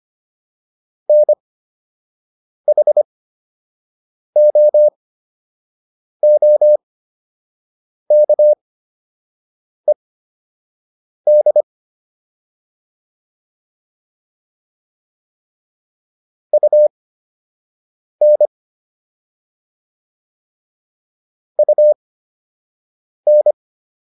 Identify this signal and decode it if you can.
Morse code, heard as a keyed tone, about 25 wpm, spaced as for 6 wpm, 600 Hz. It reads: NHOOKED UN UN